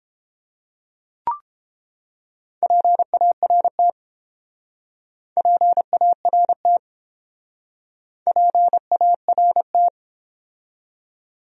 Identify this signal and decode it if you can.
Morse code, heard as a keyed tone, about 29 wpm, 700 Hz. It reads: PART PART PART